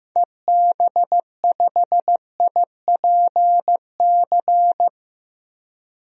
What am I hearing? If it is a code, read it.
Morse code, heard as a keyed tone, about 16 wpm, 700 Hz. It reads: EB5IPC
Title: DXCC-100 Call Signs - 6 Characters - 15wpm